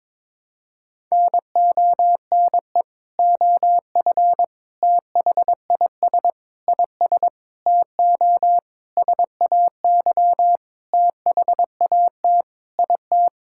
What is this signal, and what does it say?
Morse code, heard as a keyed tone, about 22 wpm, 700 Hz. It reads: NONE OF THIS IS TO SAY THAT IT
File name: Sentences with 7-Word Phrases - 22wpm